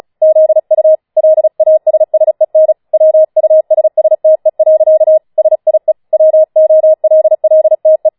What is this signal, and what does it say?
Morse code, heard as a keyed tone, about 36 wpm, 630 Hz. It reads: ZU LASSEN WUSSTE. SIE WOLLTE
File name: Schneewittchen Teil6 36wpm